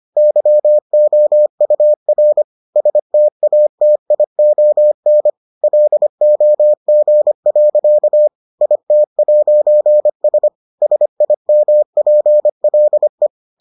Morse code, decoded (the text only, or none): YOUR STATION LOG. IT'S SIMPLE